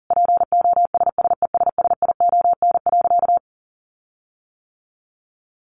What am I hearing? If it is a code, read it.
Morse code, heard as a keyed tone, about 40 wpm, 700 Hz. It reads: POSSESSION.
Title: Sentences from Top 1400 Words 40wpm